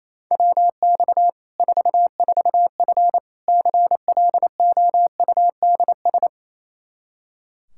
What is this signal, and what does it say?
Morse code, heard as a keyed tone, about 28 wpm, 700 Hz. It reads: WX 44F CLOUDS
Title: K1USN SST QSO Practice 28wpm